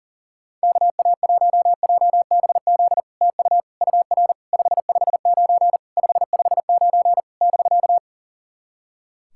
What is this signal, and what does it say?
Morse code, heard as a keyed tone, about 40 wpm, 700 Hz. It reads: KA1JBZ TU UR 559 559 <BK>
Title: SOTA QSO Practice 40wpm